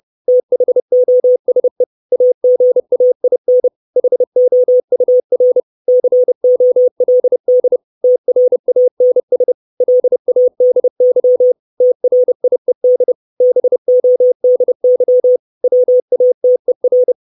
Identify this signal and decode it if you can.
Morse code, heard as a keyed tone, about 30 wpm, 500 Hz. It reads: THOSE AGAIN HOUR COLD TRANS LADY TRIED BODY WATER
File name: 632 Words 30WPM